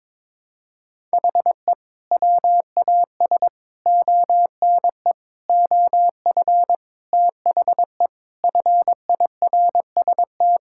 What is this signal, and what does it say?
Morse code, heard as a keyed tone, about 22 wpm, 700 Hz. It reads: HE WAS ONE OF THE FIRST
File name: practice / 22wpm / Sentences with 3-Word Phrases - 22wpm